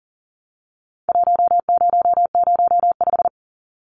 Morse code, heard as a keyed tone, about 40 wpm, 700 Hz. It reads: 1005
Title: Numbers - Contest Style - 4-Digit 40wpm